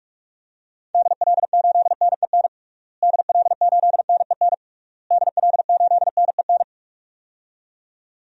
Morse code, heard as a keyed tone, about 46 wpm, 700 Hz. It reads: DL8NEN DL8NEN DL8NEN
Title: DXCC-100 Call Signs - 6 Characters - Repeated 3x - 45wpm